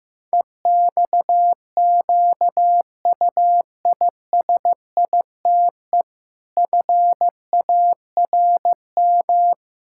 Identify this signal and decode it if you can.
Morse code, heard as a keyed tone, about 16 wpm, 700 Hz. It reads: EXQUISITE FARM
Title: Pangram Challenge v2 - 15wpm